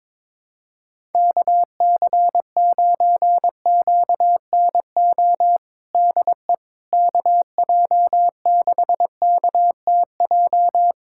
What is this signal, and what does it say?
Morse code, heard as a keyed tone, about 22 wpm, 700 Hz. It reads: KC9QNO DE KJ6KTJ
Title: USA Callsign DE Callsign 22wpm